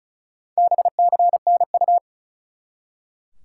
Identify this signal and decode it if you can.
Morse code, heard as a keyed tone, about 36 wpm, 700 Hz. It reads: BCNU
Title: Top 180 QSO Elements 35wpm